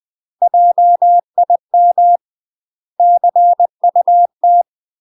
Morse code, heard as a keyed tone, about 20 wpm, 700 Hz. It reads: JIM CUT
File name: UK MASTER QSOs TRAINING/20WPM